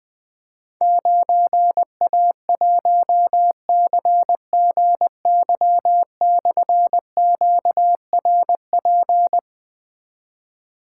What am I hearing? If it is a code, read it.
Morse code, heard as a keyed tone, about 20 wpm, 700 Hz. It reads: 9A1CGY/QRP
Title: Difficult Call Signs 20wpm